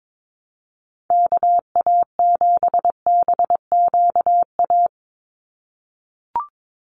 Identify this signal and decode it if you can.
Morse code, heard as a keyed tone, about 22 wpm, 700 Hz. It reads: KA7BQA
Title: USA Call Signs 22wpm